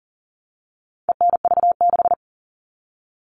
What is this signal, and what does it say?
Morse code, heard as a keyed tone, about 40 wpm, 700 Hz. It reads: ENV6
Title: All Cut Numbers - 4-Digit 40wpm